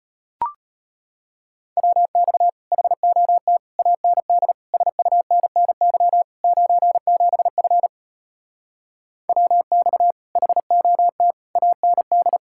Morse code, heard as a keyed tone, about 33 wpm, 700 Hz. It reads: WX HOT AND SUNNY 97F WX HOT AND